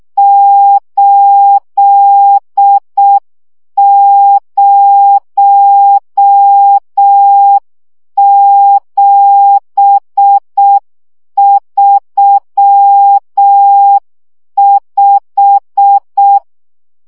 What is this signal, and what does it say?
Morse code, heard as a keyed tone, about 6 wpm, 800 Hz. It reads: 80735
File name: morze vizsga 2011-1 5